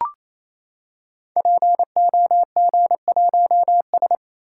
Morse code, heard as a keyed tone, about 29 wpm, 700 Hz. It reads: POG1S
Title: Code Groups - 5 Letter:Numbers - 28wpm